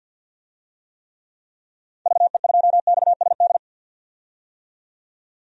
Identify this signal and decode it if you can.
Morse code, heard as a keyed tone, about 47 wpm, 700 Hz. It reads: VE2XSD
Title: DXCC-100 Call Signs - 6 Characters - 50wpm